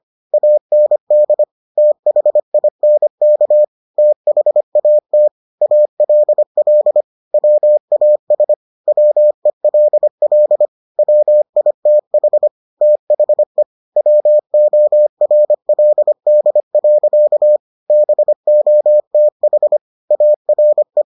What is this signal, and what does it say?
Morse code, heard as a keyed tone, about 25 wpm, 600 Hz. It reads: AND THINK THAT ALL WAS WELL WITH THE WORLD. BOTH ARE